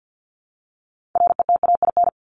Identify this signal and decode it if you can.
Morse code, heard as a keyed tone, about 50 wpm, 700 Hz. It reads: RETAIN
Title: Top 1751-2000 Words 50wpm